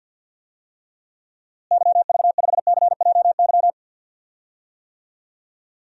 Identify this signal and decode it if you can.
Morse code, heard as a keyed tone, about 47 wpm, 700 Hz. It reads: YV5CJY
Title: DXCC-100 Call Signs - 6 Characters - Repeated 2x - 50wpm